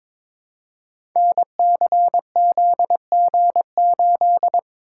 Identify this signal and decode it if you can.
Morse code, heard as a keyed tone, about 22 wpm, 700 Hz. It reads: NCZG8
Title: Code Groups - 5 Letter:Numbers - 22wpm